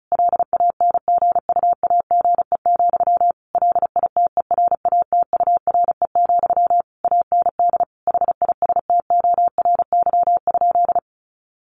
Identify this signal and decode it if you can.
Morse code, heard as a keyed tone, about 34 wpm, 700 Hz. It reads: LANGUAGE, LITERATURE, AND HISTORY?
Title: Sentences from Top 600 Words 35wpm